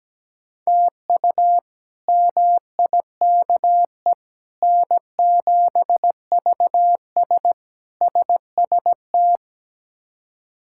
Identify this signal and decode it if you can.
Morse code, heard as a keyed tone, about 17 wpm, 700 Hz. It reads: TU MIKE N7VS SST
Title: K1USN SST QSO Practice - 17wpm